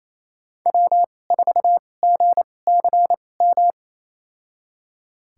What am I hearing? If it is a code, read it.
Morse code, heard as a keyed tone, about 28 wpm, spaced as for 20 wpm, 700 Hz. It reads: W4GCM
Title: USA Call Signs - 5 Characters - 2x Character Spacing - 28wpm